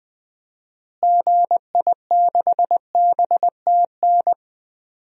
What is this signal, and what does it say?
Morse code, heard as a keyed tone, about 20 wpm, 700 Hz. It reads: GI6BTN
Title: DXCC-100 Call Signs - 6 Characters - Repeated 3x - 20wpm